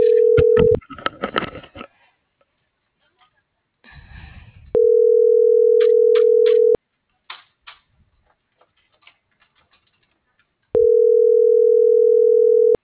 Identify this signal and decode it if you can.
Ringback tone (North America), 440 + 480 Hz.